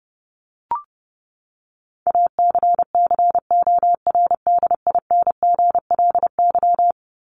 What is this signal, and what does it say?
Morse code, heard as a keyed tone, about 30 wpm, 700 Hz. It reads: ACCORDINGLY
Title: LY Suffix - Top 100 Words with Base Words 30wpm